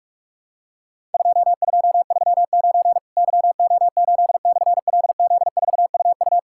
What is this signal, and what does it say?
Morse code, heard as a keyed tone, about 46 wpm, 700 Hz. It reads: 2239 YO8/LZ4UU
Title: Worked All Europe DX Contest - QTC Exchange - 45wpm